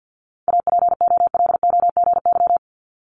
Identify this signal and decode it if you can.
Morse code, heard as a keyed tone, about 50 wpm, 700 Hz. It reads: APOLOGY
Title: Multi-letter Phonemes Repeated 3x 50wpm